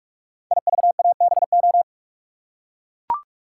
Morse code, heard as a keyed tone, about 44 wpm, 700 Hz. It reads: IVABO
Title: Code Groups - 5 Letters - 45wpm